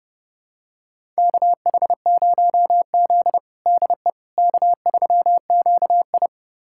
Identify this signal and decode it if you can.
Morse code, heard as a keyed tone, about 30 wpm, 700 Hz. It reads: KH0Z DE K3QI
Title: USA Callsign DE Callsign - 4 Characters - 30wpm